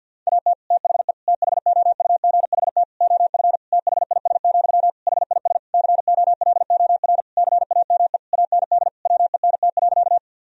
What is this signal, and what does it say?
Morse code, heard as a keyed tone, about 50 wpm, 700 Hz. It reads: AT THE THOUGHT OF THIS, HIS COLOR CAME AND WENT.
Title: Sentences from Top 1000 Words - Review Part 2 - 50wpm